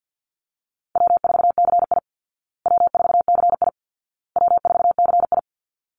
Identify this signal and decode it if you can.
Morse code, heard as a keyed tone, about 50 wpm, 700 Hz. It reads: W4CI W4CI W4CI
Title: Call Signs Repeated 2x - 50wpm